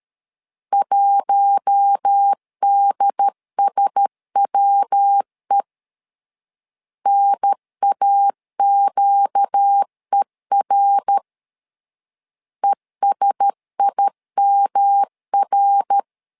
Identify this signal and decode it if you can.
Morse code, heard as a keyed tone, about 13 wpm, 800 Hz. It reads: 1DSWE NAQER ESIMR
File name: LEZIONI / 4 Q1S 12wpm 1802